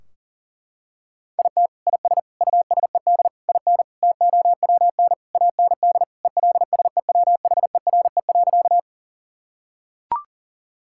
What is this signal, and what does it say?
Morse code, heard as a keyed tone, about 40 wpm, 700 Hz. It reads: IT IS USED IN TOWN AND ELSEWHERE.
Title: Sentences from Top 1600 Words 40wpm